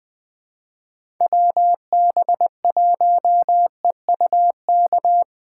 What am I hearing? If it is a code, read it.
Morse code, heard as a keyed tone, about 23 wpm, 700 Hz. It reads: WB1EUK
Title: Speed Racing - USA 2x3 Call Signs 20wpm